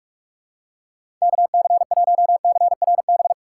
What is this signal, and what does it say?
Morse code, heard as a keyed tone, about 45 wpm, 700 Hz. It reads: KC1CRB
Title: USA Call Signs - 6 Characters - 6x Character Spacing - 45wpm